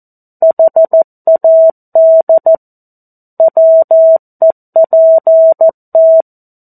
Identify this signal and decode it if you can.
Morse code, heard as a keyed tone, about 14 wpm, 650 Hz. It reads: HAD WEPT
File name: 02-Chapter-02 14 WPM